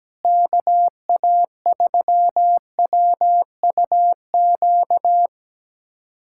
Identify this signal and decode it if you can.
Morse code, heard as a keyed tone, about 17 wpm, 700 Hz. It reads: KA3WUQ
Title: USA Callsign DE Callsign - 6 Characters - 17wpm